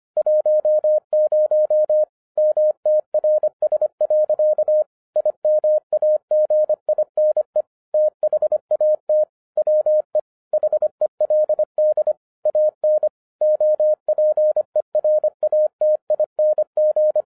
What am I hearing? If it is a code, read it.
Morse code, heard as a keyed tone, about 26 wpm, 610 Hz. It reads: 10 MTRS. IMAGINE THAT WE HELD AN OPERATING
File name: 2012-01-mp3 / n14-FUN WITH NUMBERS-25wpm